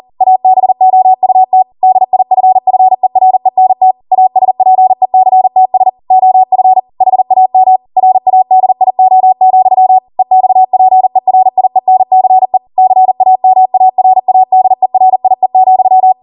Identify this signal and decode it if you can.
Morse code, heard as a keyed tone, about 40 wpm, 750 Hz. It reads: ABOUT DIFFERENT ASPECTS OF HAM RADIO, EXPERIENCE CAMARADERIE,